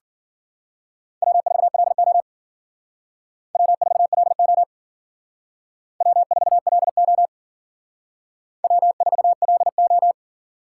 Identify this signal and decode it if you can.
Morse code, heard as a keyed tone, about 47 wpm, 700 Hz. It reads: W4LO W4LO W4LO W4LO
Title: Speed Racing - USA 1x2 Call Signs 40wpm